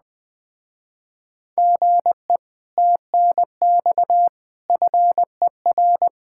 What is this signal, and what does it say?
Morse code, heard as a keyed tone, about 20 wpm, 700 Hz. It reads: GE TNX FER
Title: Speed-Racing Common QSO Phrases - 17wpm